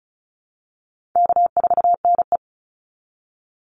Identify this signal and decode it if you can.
Morse code, heard as a keyed tone, about 34 wpm, 700 Hz. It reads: K4NE
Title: Easy Call Signs Repeated 2x - 35wpm